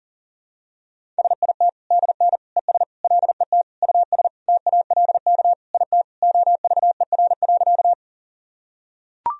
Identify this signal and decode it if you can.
Morse code, heard as a keyed tone, about 40 wpm, 700 Hz. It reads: SIT DN ES LET US TALK IT OVER.
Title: Sentences from Top 500 Words and Abbreviations 40wpm